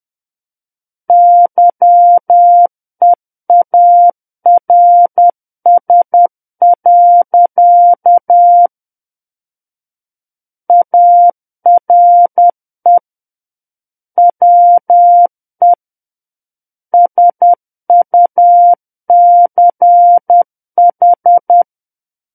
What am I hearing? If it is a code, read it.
Morse code, heard as a keyed tone, about 10 wpm, 700 Hz. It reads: YEARS. ARE WE SUCH
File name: Book One / 01-Chapter 1 - 10 WPM